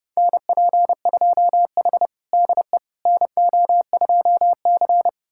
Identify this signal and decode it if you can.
Morse code, heard as a keyed tone, about 32 wpm, 700 Hz. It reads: NP2H DE NO2C
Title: USA Callsign DE Callsign - 4 Characters - 30wpm